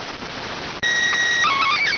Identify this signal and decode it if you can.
SSTV (slow-scan television), PD 290.